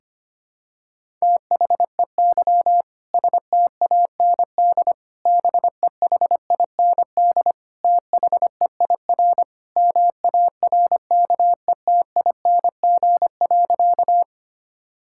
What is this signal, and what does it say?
Morse code, heard as a keyed tone, about 25 wpm, 700 Hz. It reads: THEY STAND BEHIND THEIR MARKETING.
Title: Sentences from Top 1000 Words - Review Part 5 - 25wpm